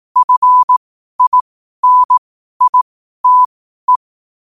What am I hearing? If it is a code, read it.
Morse code, heard as a keyed tone, about 18 wpm, spaced as for 12 wpm, 1000 Hz. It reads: FINITE
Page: New quotes every day in morse code at 12 Words per minute.